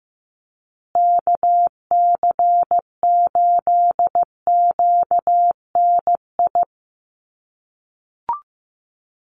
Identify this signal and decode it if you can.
Morse code, heard as a keyed tone, about 15 wpm, 700 Hz. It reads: KC8QNI
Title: USA Call Signs Repeated 2x 15wpm